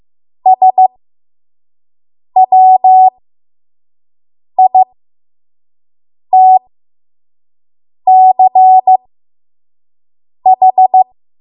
Morse code, heard as a keyed tone, about 15 wpm, spaced as for 5 wpm, 750 Hz. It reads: SWITCH
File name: Morse / 05WPM